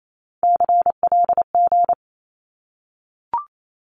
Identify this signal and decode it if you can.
Morse code, heard as a keyed tone, about 28 wpm, 700 Hz. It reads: CLG
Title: Abbreviations 28wpm